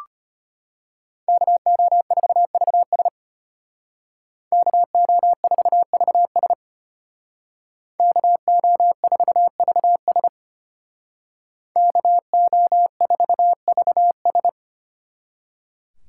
Morse code, heard as a keyed tone, about 30 wpm, 700 Hz. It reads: KO4VS KO4VS KO4VS KO4VS
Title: Speed Racing - USA Call Signs 25wpm